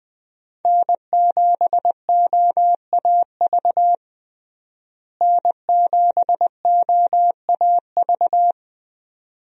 Morse code, heard as a keyed tone, about 20 wpm, 700 Hz. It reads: N7OAV N7OAV
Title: USA Call Signs - 5 Characters - Repeated 2x - 20wpm